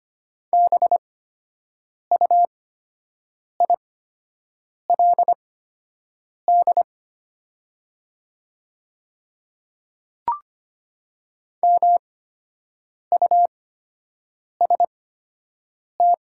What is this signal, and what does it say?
Morse code, heard as a keyed tone, about 25 wpm, spaced as for 7 wpm, 700 Hz. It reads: BUILD MUST
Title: 8x Character Spacing - Top 300 Words - 25wpm